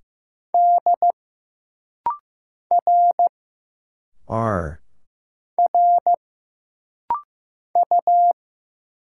Morse code, heard as a keyed tone, about 15 wpm, 700 Hz. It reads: D R R U
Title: Single Letters - Rapid-Fire 15wpm